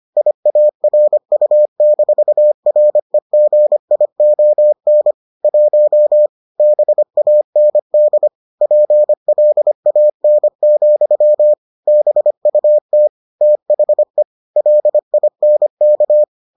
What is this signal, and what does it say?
Morse code, heard as a keyed tone, about 25 wpm, 600 Hz. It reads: IARU-REGION 1 BAND PLAN, BUT THE LINK